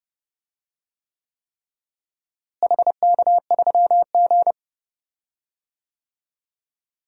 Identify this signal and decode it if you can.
Morse code, heard as a keyed tone, about 30 wpm, 700 Hz. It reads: HK3G
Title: DXCC-100 Call Signs - 4 Characters - Repeated 3x - 30wpm